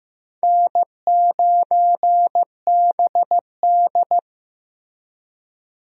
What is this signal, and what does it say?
Morse code, heard as a keyed tone, about 15 wpm, 700 Hz. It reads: N9BD
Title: USA Call Signs - 4 Characters - Repeated 2x - 15wpm